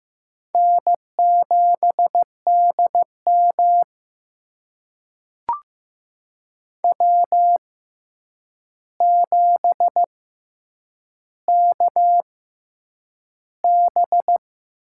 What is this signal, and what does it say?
Morse code, heard as a keyed tone, about 15 wpm, spaced as for 8 wpm, 700 Hz. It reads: N7DM W7KB
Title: USA Call Signs - 4 Characters - 6x Character Spacing - 15wpm